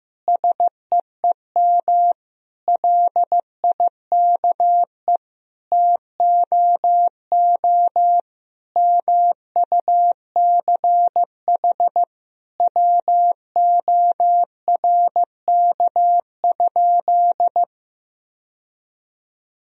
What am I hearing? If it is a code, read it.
Morse code, heard as a keyed tone, about 15 wpm, 700 Hz. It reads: SEEM LIKE TOO MUCH WORK?
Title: Sentences from Top 1000 Words - Review Part 1 - 15wpm